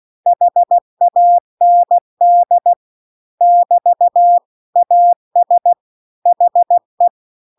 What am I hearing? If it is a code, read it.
Morse code, heard as a keyed tone, about 16 wpm, 700 Hz. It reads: HAND <BT> AS HE